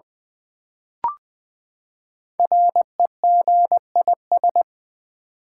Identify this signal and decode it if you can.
Morse code, heard as a keyed tone, about 20 wpm, 700 Hz. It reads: REGIS
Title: Build A Word - 5 Letters At A Time - 12-Letter Words - 20wpm